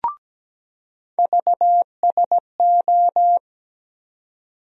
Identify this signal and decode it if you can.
Morse code, heard as a keyed tone, about 17 wpm, 700 Hz. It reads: VSO
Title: USA Call Sign Postfixes - 3 Letters 17wpm